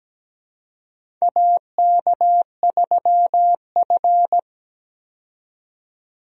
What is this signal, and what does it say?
Morse code, heard as a keyed tone, about 17 wpm, 700 Hz. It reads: AK3F
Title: USA Call Signs - 4 Characters - 17wpm